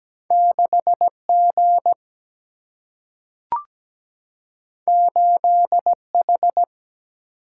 Morse code, heard as a keyed tone, about 17 wpm, 700 Hz. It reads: 6G 8H
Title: Single-Letter Single-Number 17wpm